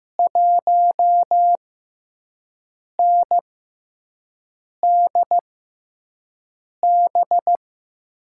Morse code, heard as a keyed tone, about 15 wpm, spaced as for 5 wpm, 700 Hz. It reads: 1NDB
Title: USA Call Signs - 6x Character Spacing - 15wpm